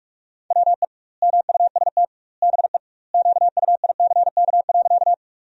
Morse code, heard as a keyed tone, about 46 wpm, 700 Hz. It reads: WE MUST BE QUICK.
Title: 2x Word-Spacing - Sentences from Top 500 Words - 45wpm